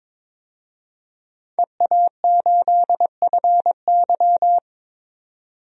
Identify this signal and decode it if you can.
Morse code, heard as a keyed tone, about 21 wpm, 700 Hz. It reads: EA8FY
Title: DXCC-100 Call Signs - 5 Characters - 22wpm